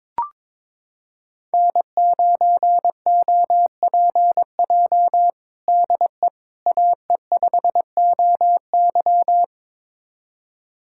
Morse code, heard as a keyed tone, about 22 wpm, 700 Hz. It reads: N9OPJ DE AE5OY
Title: USA Callsign DE Callsign - 5 Characters - 22wpm